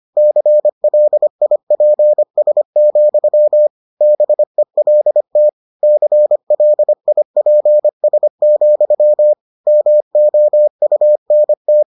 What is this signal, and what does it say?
Morse code, heard as a keyed tone, about 25 wpm, 600 Hz. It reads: CLIPS, BELT CLIPS, MOUNT